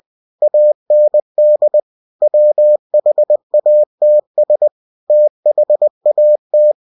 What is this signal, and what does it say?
Morse code, heard as a keyed tone, about 20 wpm, 600 Hz. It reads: AND WHATS THAT
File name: cw-books / winnie-the-pooh / winnie-20wpm-page-0053